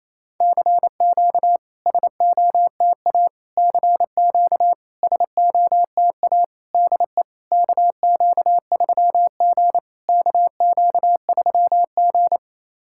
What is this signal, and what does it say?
Morse code, heard as a keyed tone, about 28 wpm, 690 Hz. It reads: CQ SOTA CQ SOTA DE KQ3G KQ3G
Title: SOTA QSO Practice 28wpm